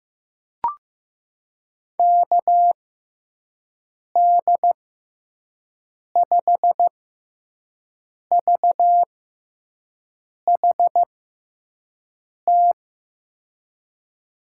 Morse code, heard as a keyed tone, about 15 wpm, spaced as for 5 wpm, 700 Hz. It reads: KD5VHT
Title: USA Call Signs - 6 Characters - 6x Character Spacing - 15wpm